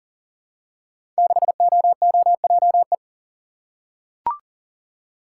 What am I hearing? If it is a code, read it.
Morse code, heard as a keyed tone, about 39 wpm, 700 Hz. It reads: 6OOJE
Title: Code Groups - 5 Letter:Numbers - 40wpm